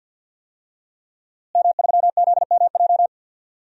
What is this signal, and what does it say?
Morse code, heard as a keyed tone, about 50 wpm, 700 Hz. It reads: M3ZMJ
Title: DXCC-100 Call Signs - 5 Characters - Repeated 3x - 50wpm